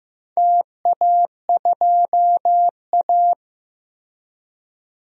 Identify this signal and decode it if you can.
Morse code, heard as a keyed tone, about 15 wpm, 700 Hz. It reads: TA2A
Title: DXCC-100 Call Signs - 4 Characters - Repeated 2x - 15wpm